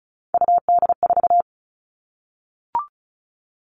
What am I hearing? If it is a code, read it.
Morse code, heard as a keyed tone, about 34 wpm, 700 Hz. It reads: UD4
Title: All Cut Numbers - 3-Digit 35wpm